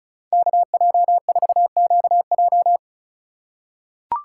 Morse code, heard as a keyed tone, about 36 wpm, 700 Hz. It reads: KJ4QJ
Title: USA Call Signs - 5 Characters - 3x Character Spacing - 35wpm